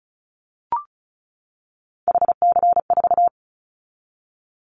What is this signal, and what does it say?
Morse code, heard as a keyed tone, about 36 wpm, 700 Hz. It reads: HC4
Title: Callsign Prefixes for Top 100 DXCC - 35wpm